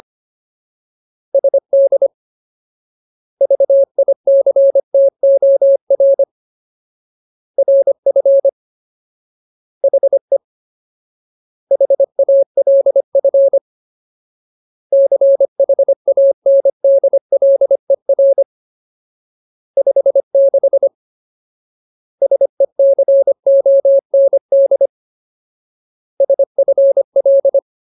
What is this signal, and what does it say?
Morse code, heard as a keyed tone, about 25 wpm, 550 Hz. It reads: SD VICTOR RF HE HALF CHANDLER 56 SECOND SFL
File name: Ultimate Word List / 25WPM Ultimate